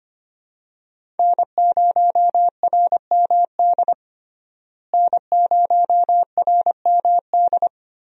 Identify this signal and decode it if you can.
Morse code, heard as a keyed tone, about 25 wpm, 700 Hz. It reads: N0RMD N0RMD
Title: USA Call Signs - 5 Characters - Repeated 3x - 25wpm